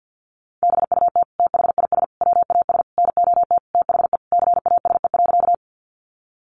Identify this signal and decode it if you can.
Morse code, heard as a keyed tone, about 48 wpm, 700 Hz. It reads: BUT THIS WAS NOT THE CASE.
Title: Sentences from Top 700 Words 50wpm